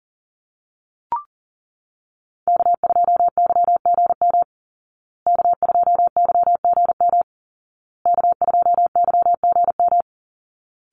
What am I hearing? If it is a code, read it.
Morse code, heard as a keyed tone, about 40 wpm, 700 Hz. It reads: K2YGM K2YGM K2YGM
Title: Call Signs Repeated 2x - 40wpm